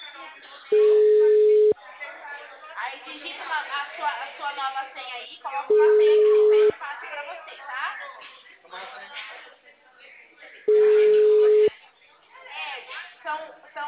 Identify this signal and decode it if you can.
Ringback tone (Europe, 425 Hz), with speech.